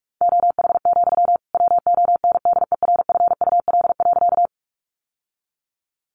Morse code, heard as a keyed tone, about 44 wpm, 700 Hz. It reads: OH, WONDERFUL.
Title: Sentences from Top 1100 Words 45wpm